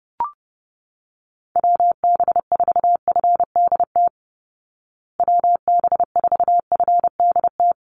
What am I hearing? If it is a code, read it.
Morse code, heard as a keyed tone, about 30 wpm, 700 Hz. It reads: WB4FDT WB4FDT
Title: Call Signs Repeated 2x - 30wpm